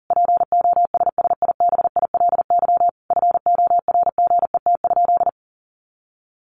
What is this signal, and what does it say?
Morse code, heard as a keyed tone, about 40 wpm, 700 Hz. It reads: POSSIBILY FORGET?
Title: Sentences from Top 1300 Words 40wpm